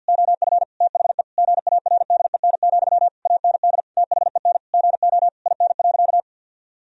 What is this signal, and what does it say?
Morse code, heard as a keyed tone, about 50 wpm, 700 Hz. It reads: OF THE GARDEN, AND THEN GO IN.
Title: Sentences from Top 1000 Words - Review Part 4 - 50wpm